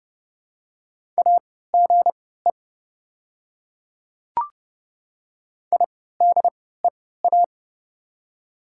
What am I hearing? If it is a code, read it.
Morse code, heard as a keyed tone, about 30 wpm, spaced as for 17 wpm, 690 Hz. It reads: AGE IDEA
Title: 3x Character Spacing - Top 500 Words - 30wpm